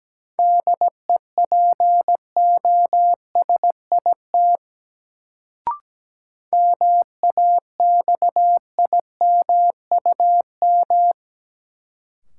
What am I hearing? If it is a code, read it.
Morse code, heard as a keyed tone, about 17 wpm, 700 Hz. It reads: DEPOSIT MAXIMUM
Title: Sentences from Top 1300 Words - New Word Drill - 17wpm